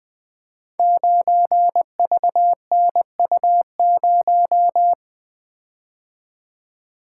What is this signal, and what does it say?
Morse code, heard as a keyed tone, about 20 wpm, 700 Hz. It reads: 9VNU0
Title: Code Groups - 5 Letter:Numbers - 20wpm